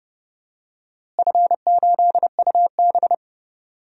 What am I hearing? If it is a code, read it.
Morse code, heard as a keyed tone, about 30 wpm, 700 Hz. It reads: F8UB
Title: DXCC-100 Call Signs - 4 Characters - Repeated 2x - 30wpm